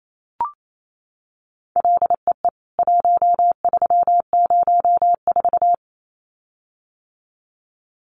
Morse code, heard as a keyed tone, about 28 wpm, 700 Hz. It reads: LEE 1304
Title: CWT Exchange 28wpm